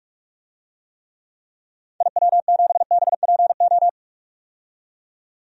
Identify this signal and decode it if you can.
Morse code, heard as a keyed tone, about 46 wpm, 700 Hz. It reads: IW7BPO
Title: DXCC-100 Call Signs - 6 Characters - 45wpm